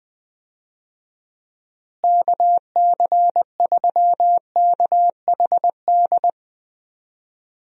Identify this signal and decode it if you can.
Morse code, heard as a keyed tone, about 21 wpm, 700 Hz. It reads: KC3KHD
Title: USA Call Signs - 6 Characters - Repeated 2x - 20wpm